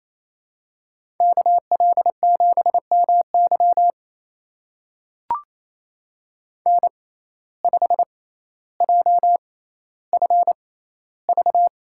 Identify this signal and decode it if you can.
Morse code, heard as a keyed tone, about 28 wpm, spaced as for 14 wpm, 700 Hz. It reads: KL7MY N5JFV
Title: USA Call Signs - 5 Characters - 6x Character Spacing - 28wpm